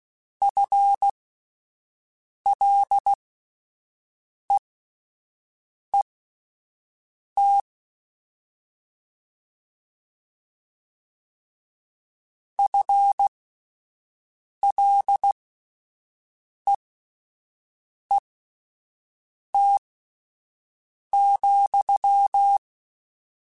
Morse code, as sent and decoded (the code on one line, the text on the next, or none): ..-. .-.. . - / ..-. .-.. . - --..--
FLEET FLEET,